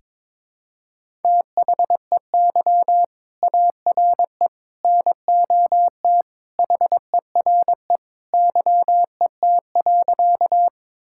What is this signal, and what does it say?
Morse code, heard as a keyed tone, about 21 wpm, 700 Hz. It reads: THEY ARE NOT HERE YET.
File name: practice / 22wpm / Sentences with 2-Word Phrases - 22wpm